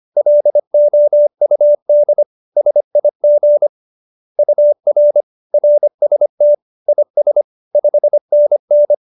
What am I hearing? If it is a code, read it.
Morse code, heard as a keyed tone, about 25 wpm, 590 Hz. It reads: LOUD SIG UR RST IS 5NN
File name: QSO318 25